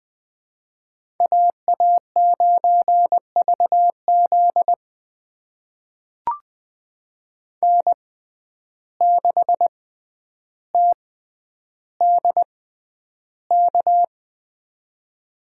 Morse code, heard as a keyed tone, about 20 wpm, spaced as for 10 wpm, 700 Hz. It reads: AA9VZ N6TDK
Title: USA Call Signs - 5 Characters - 6x Character Spacing - 20wpm